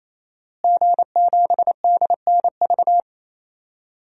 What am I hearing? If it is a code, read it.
Morse code, heard as a keyed tone, about 28 wpm, 700 Hz. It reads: G7DNV
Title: DXCC-100 Call Signs - 5 Characters - Repeated 3x - 28wpm